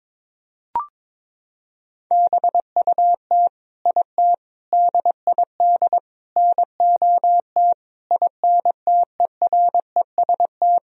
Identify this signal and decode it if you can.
Morse code, heard as a keyed tone, about 22 wpm, 700 Hz. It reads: BUT IT DID NOT INTEREST
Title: Sentences from Top 1000 Words - Review Part 2 - 22wpm